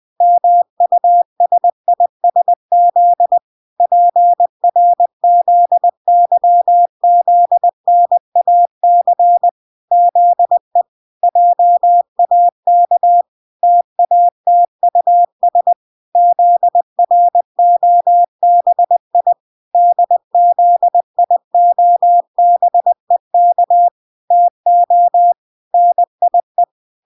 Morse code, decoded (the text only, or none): MUSISZ PRZYZNAC ZE JAK TATUS ZROBI DZIOBEK TO NIE